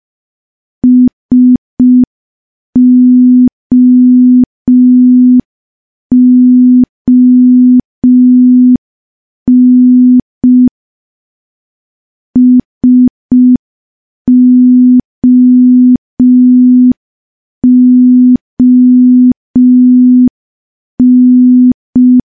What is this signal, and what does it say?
Morse code, heard as a keyed tone, about 5 wpm, 250 Hz. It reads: SOON SOON
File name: morse / morse-training-05WPM